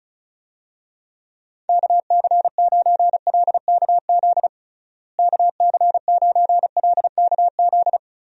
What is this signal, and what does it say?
Morse code, heard as a keyed tone, about 34 wpm, 700 Hz. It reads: KC9LKZ KC9LKZ
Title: USA Call Signs - 6 Characters - Repeated 2x - 35wpm